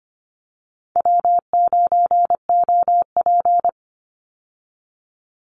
Morse code, heard as a keyed tone, about 25 wpm, 700 Hz. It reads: W9OP
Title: Call Signs - 25wpm